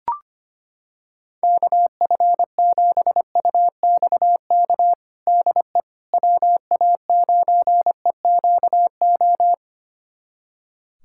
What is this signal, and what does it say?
Morse code, heard as a keyed tone, about 25 wpm, 700 Hz. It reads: KF7UXK DE WA9EQO
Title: USA Callsign DE Callsign 25wpm